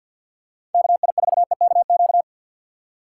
Morse code, heard as a keyed tone, about 53 wpm, 700 Hz. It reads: KI4EKQ
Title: USA Call Signs - 6 Characters - 4x Character Spacing - 50wpm